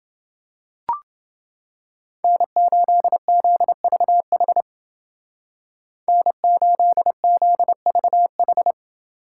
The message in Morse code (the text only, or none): N8ZVH N8ZVH